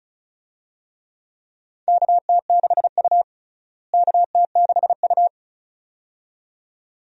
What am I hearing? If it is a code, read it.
Morse code, heard as a keyed tone, about 36 wpm, 700 Hz. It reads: KT6U KT6U
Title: USA Call Signs - 4 Characters - Repeated 2x - 35wpm